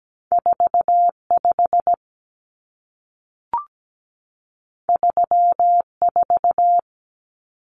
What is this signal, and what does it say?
Morse code, heard as a keyed tone, about 17 wpm, 700 Hz. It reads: 45 34
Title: Numbers - 2-Digit 17wpm